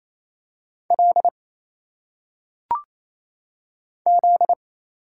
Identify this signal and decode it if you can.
Morse code, heard as a keyed tone, about 27 wpm, 700 Hz. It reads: L Z
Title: Single Letters - Phonetic Pronunciation 28wpm